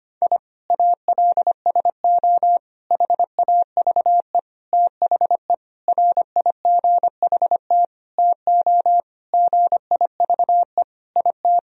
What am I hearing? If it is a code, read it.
Morse code, heard as a keyed tone, about 25 wpm, 700 Hz. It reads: I ALSO HAVE THE RIGHT TO GIVE IT
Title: Sentences from Top 1000 Words - Review Part 1 - 25wpm